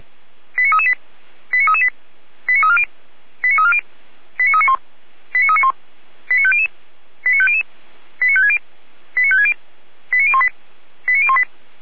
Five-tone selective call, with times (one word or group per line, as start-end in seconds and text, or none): 0.6-0.9 89338
1.5-1.9 89338
2.5-2.8 89340
3.4-3.8 89340
4.4-4.8 89391
5.3-5.7 89391
6.3-6.7 89500
7.3-7.6 89500
8.2-8.6 89560
9.2-9.5 89560
10.1-10.5 89018
11.1-11.4 89018